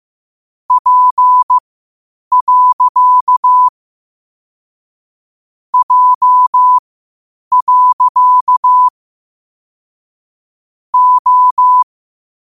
Morse code, heard as a keyed tone, about 15 wpm, spaced as for 8 wpm, 1000 Hz. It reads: P. J. O
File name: qotdfiles / 8wpm